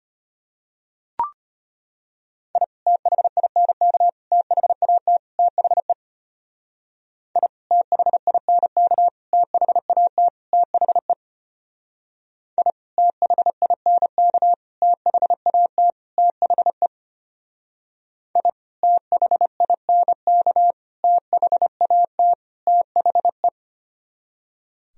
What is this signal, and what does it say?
Morse code, heard as a keyed tone, about 31 wpm, 700 Hz. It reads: I THINK THAT THE I THINK THAT THE I THINK THAT THE I THINK THAT THE
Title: Speed-Racing - Sets of 4 Words 25wpm